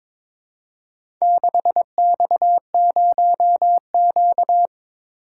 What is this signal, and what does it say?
Morse code, heard as a keyed tone, about 22 wpm, 700 Hz. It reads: 6X0Q
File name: practice / 22wpm / Four Letter-Number Combinations - 22wpm